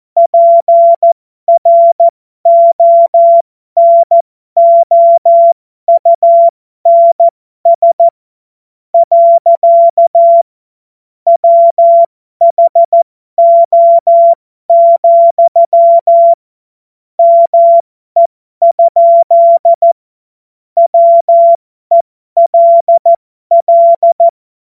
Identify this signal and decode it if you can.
Morse code, heard as a keyed tone, about 14 wpm, spaced as for 12 wpm, 680 Hz. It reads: PRONOUNS. WHO, ME? WELL
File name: Morsekurs / Witze